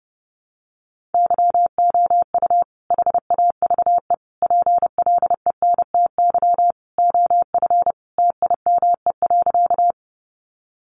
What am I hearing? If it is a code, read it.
Morse code, heard as a keyed tone, about 30 wpm, 700 Hz. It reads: YOU HAVE PLENTY OF TIME.
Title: Sentences from Top 1300 Words 30wpm